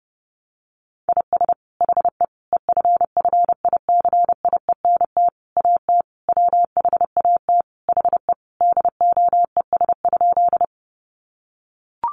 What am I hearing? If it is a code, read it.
Morse code, heard as a keyed tone, about 30 wpm, 700 Hz. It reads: IS HE EFFICIENT AT WHAT HE DOES?
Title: Sentences from Top 1300 Words 30wpm